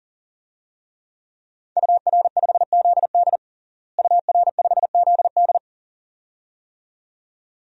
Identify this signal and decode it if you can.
Morse code, heard as a keyed tone, about 40 wpm, 700 Hz. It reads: UR5ZD UR5ZD
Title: DXCC-100 Call Signs - 5 Characters - Repeated 2x - 40wpm